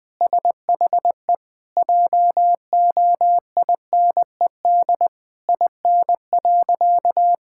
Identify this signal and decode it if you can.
Morse code, heard as a keyed tone, about 20 wpm, 700 Hz. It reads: SHE JOINED IN.
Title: Sentences with 2-Word Phrases - 20wpm